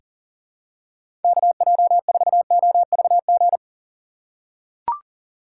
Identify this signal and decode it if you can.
Morse code, heard as a keyed tone, about 39 wpm, 700 Hz. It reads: KJ4OVG
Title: USA Call Signs - 6 Characters - 3x Character Spacing - 40wpm